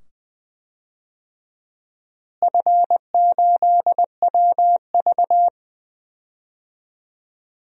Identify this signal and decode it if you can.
Morse code, heard as a keyed tone, about 20 wpm, 700 Hz. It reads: F8WV
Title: DXCC-100 Call Signs - 4 Characters - Repeated 2x - 20wpm